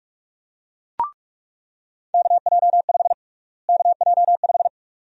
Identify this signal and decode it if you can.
Morse code, heard as a keyed tone, about 46 wpm, 700 Hz. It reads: KJ5 KJ5
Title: USA Call Sign Prefixes - Repeated 2x - 45wpm